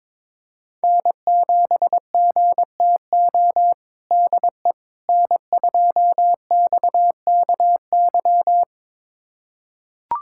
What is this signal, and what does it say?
Morse code, heard as a keyed tone, about 22 wpm, 700 Hz. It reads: N7GTO DE N2XKY
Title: USA Callsign DE Callsign - 5 Characters - 22wpm